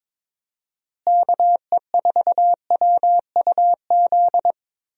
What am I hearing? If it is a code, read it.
Morse code, heard as a keyed tone, about 21 wpm, 700 Hz. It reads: KE4WUZ
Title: USA Call Signs - 6 Characters - Repeated 3x - 22wpm